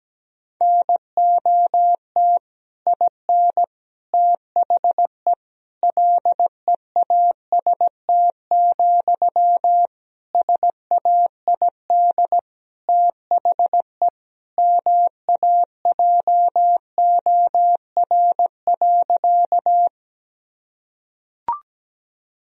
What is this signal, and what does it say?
Morse code, heard as a keyed tone, about 17 wpm, 700 Hz. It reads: NOT IN THE LEAST, SAID THE MAJOR.
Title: Sentences from Top 1000 Words - Review Part 4 - 17wpm